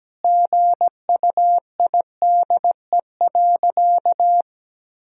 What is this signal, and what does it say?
Morse code, heard as a keyed tone, about 17 wpm, 700 Hz. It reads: GUIDE.
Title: Sentences from Top 1000 Words - Review Part 3 - 17wpm